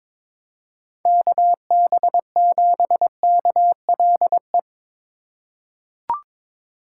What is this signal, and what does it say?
Morse code, heard as a keyed tone, about 22 wpm, 700 Hz. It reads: KB7KLE
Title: USA Call Signs - 6 Characters - 4x Character Spacing - 22wpm